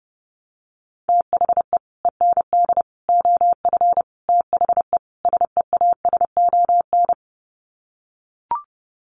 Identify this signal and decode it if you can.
Morse code, heard as a keyed tone, about 30 wpm, 700 Hz. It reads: THE END OF THE SEASON
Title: Sets of 5 Words Encore 30wpm